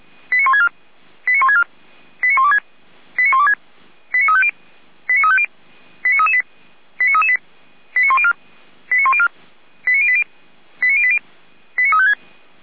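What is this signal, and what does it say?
Five-tone selective call (ZVEI), heard as quick runs of five tones, tones of about 70 ms. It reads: 89164 89164 89126 89126 89350 89350 89338 89338 89104 89104 89080 89080 89356
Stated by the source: Five-tone ZVEI selective call received on an SDR.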